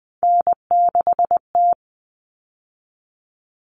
Morse code, heard as a keyed tone, about 20 wpm, 700 Hz. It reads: N6T
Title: All Cut Numbers - 3-Digit 20wpm